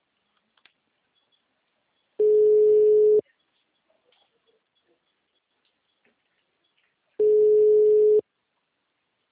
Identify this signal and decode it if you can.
Ringback tone (Europe, 425 Hz).